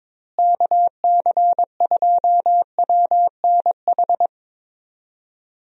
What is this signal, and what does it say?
Morse code, heard as a keyed tone, about 22 wpm, 700 Hz. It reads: KC2WNH
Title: USA Call Signs - 6 Characters - Repeated 3x - 22wpm